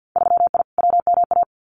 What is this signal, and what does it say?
Morse code, heard as a keyed tone, about 48 wpm, 700 Hz. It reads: 3I WMA
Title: Winter Field Day Exchanges 50wpm